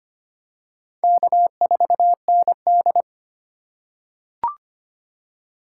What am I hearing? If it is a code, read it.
Morse code, heard as a keyed tone, about 25 wpm, 700 Hz. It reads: K4ND
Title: USA Call Signs - 2x Character Spacing - 25wpm